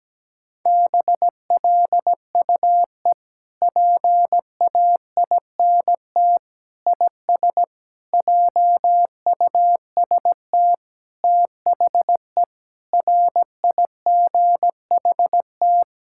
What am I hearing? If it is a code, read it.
Morse code, heard as a keyed tone, about 17 wpm, 700 Hz. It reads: BLUE PAINT IS JUST THE RIGHT
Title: Sentences from Top 1000 Words - Review Part 6 - 17wpm